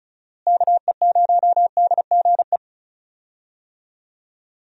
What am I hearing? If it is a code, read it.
Morse code, heard as a keyed tone, about 36 wpm, 700 Hz. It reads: KE0DGE
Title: USA Call Signs - 6 Characters - Repeated 2x - 35wpm